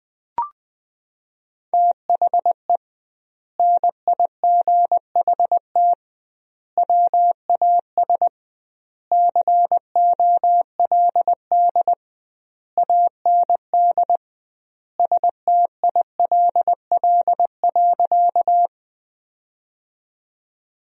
Morse code, heard as a keyed tone, about 20 wpm, 700 Hz. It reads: THE NIGHT WAS COLD AND STILL.
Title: 2x Word-Spacing - Sentences from Top 400 Words - 20wpm